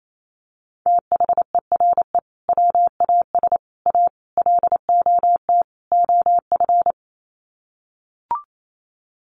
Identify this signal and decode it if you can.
Morse code, heard as a keyed tone, about 28 wpm, 700 Hz. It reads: THERE WAS A LOT OF
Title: Sets of 5 Words 28wpm